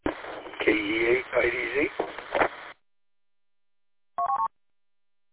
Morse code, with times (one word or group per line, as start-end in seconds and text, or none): none